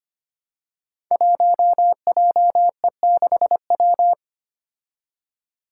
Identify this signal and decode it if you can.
Morse code, heard as a keyed tone, about 25 wpm, 700 Hz. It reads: 1JE6W
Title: Five Challenging Letter-Number Combinations 25wpm